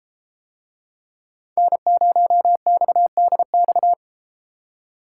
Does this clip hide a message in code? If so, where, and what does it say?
Yes: Morse code from 1.6 to 3.9 s, heard as a keyed tone, about 32 wpm, 700 Hz. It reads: N0XDX